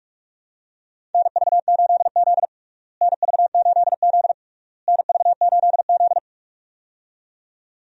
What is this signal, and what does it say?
Morse code, heard as a keyed tone, about 41 wpm, 700 Hz. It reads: NV8Z NV8Z NV8Z
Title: USA Call Signs - 4 Characters - Repeated 3x - 45wpm